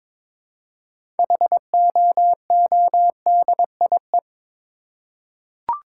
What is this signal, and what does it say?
Morse code, heard as a keyed tone, about 22 wpm, 690 Hz. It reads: HOODIE
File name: Difficult Words with Hints - 22wpm